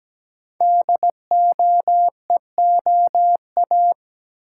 Morse code, heard as a keyed tone, about 17 wpm, 700 Hz. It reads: DOEOA
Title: Code Groups - 5 Letters - 17wpm